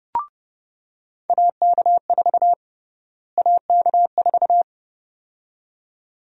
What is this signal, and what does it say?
Morse code, heard as a keyed tone, about 30 wpm, 700 Hz. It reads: AK4 AK4
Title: USA Call Sign Prefixes - Repeated 2x - 30wpm